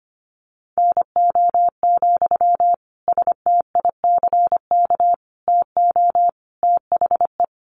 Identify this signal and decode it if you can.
Morse code, heard as a keyed tone, about 25 wpm, 700 Hz. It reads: NO, STICK TO THE
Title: Sentences from Top 1400 Words 25wpm